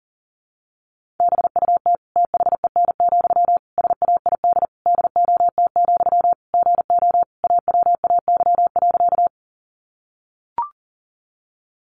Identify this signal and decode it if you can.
Morse code, heard as a keyed tone, about 40 wpm, 700 Hz. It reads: BUT THEN, SAID DOT, GO AWAY.
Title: Sentences from Top 1500 Words 40wpm